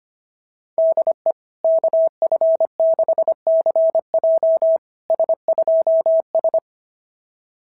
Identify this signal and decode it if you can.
Morse code, heard as a keyed tone, about 24 wpm, 650 Hz. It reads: DE KF6CJ S2S
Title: SOTA QSO Practice 25wpm